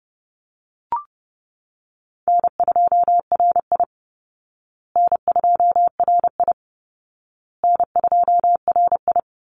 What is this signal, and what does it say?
Morse code, heard as a keyed tone, about 30 wpm, 700 Hz. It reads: N2RI N2RI N2RI
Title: Call Signs Repeated 2x - 30wpm